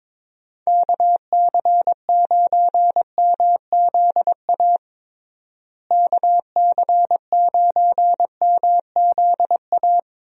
Morse code, heard as a keyed tone, about 21 wpm, 700 Hz. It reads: KC9MZA KC9MZA
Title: USA Call Signs - 6 Characters - Repeated 3x - 22wpm